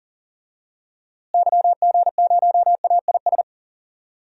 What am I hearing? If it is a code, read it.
Morse code, heard as a keyed tone, about 40 wpm, 700 Hz. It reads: YG0AIS
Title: DXCC-100 Call Signs - 6 Characters - Repeated 3x - 40wpm